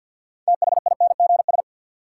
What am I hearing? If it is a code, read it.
Morse code, heard as a keyed tone, about 53 wpm, 700 Hz. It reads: THINGS
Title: Sentences from Top 1400 Words - New Word Drill Alt - 50wpm